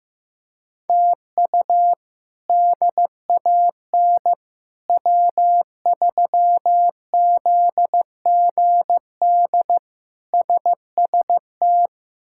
Morse code, heard as a keyed tone, about 15 wpm, 700 Hz. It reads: TU DAN W3ZGD SST
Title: K1USN SST QSO Practice - 15wpm